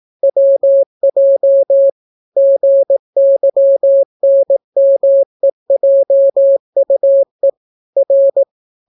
Morse code, heard as a keyed tone, about 18 wpm, 550 Hz. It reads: WJ GYNMEJUE R